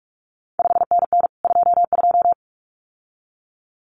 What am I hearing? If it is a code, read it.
Morse code, heard as a keyed tone, about 46 wpm, 700 Hz. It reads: 5NN 22
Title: CQ WW DX Contest Exchange Encore - 45wpm